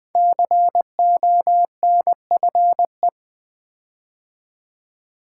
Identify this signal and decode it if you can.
Morse code, heard as a keyed tone, about 20 wpm, 700 Hz. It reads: CONFE
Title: Build A Word - 5 Letters At A Time - 10-Letter Words - 20wpm